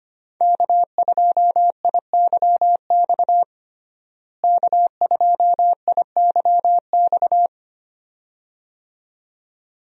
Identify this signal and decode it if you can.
Morse code, heard as a keyed tone, about 25 wpm, 700 Hz. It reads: K2IYX K2IYX
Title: USA Call Signs - 5 Characters - Repeated 2x - 25wpm